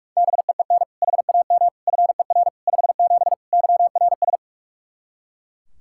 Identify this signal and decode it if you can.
Morse code, heard as a keyed tone, about 41 wpm, 700 Hz. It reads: BEEN HAM FER 57 YRS
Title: K1USN SST QSO Practice 45wpm